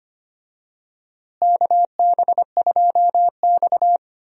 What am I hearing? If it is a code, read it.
Morse code, heard as a keyed tone, about 25 wpm, 700 Hz. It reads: KB2X
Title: USA Call Signs - 4 Characters - Repeated 3x - 25wpm